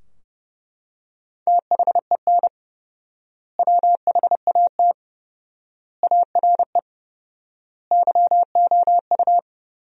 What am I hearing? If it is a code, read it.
Morse code, heard as a keyed tone, about 30 wpm, 700 Hz. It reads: THEN WHAT ARE YOU
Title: 4x Word-Spacing - Sentences from Top 100 Words - 30wpm